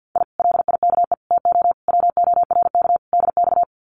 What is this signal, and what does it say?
Morse code, heard as a keyed tone, about 48 wpm, 700 Hz. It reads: I LIKE TO WORK DX
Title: Common QSO Phrases Encore 50wpm